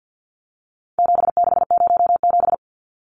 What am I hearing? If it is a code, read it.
Morse code, heard as a keyed tone, about 50 wpm, 700 Hz. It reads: 7607
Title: Numbers - 4-Digit 50wpm